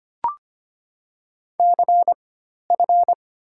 Morse code, heard as a keyed tone, about 25 wpm, 700 Hz. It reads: C F
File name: Top 90 QSO Elements - 4x Character Spacing - 25wpm